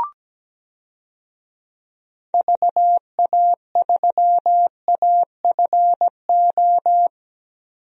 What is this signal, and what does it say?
Morse code, heard as a keyed tone, about 17 wpm, 700 Hz. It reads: VA3AFO
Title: DXCC-100 Call Signs - 6 Characters - 17wpm